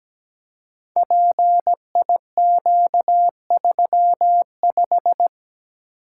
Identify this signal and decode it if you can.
Morse code, heard as a keyed tone, about 17 wpm, 700 Hz. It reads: PIQ35
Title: Code Groups - 5 Letter:Numbers - 17wpm